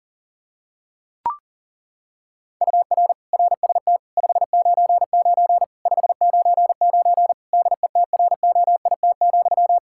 Morse code, heard as a keyed tone, about 40 wpm, 700 Hz. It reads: UR RST 599 599 DETROIT,
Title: Practice QSO - Exchange 1 - Beginning - 40wpm